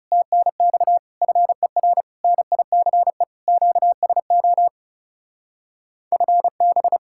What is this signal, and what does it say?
Morse code, heard as a keyed tone, about 32 wpm, 700 Hz. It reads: TNX FER NICE QSO FB
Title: Speed-Racing Common QSO Phrases Encore - 30wpm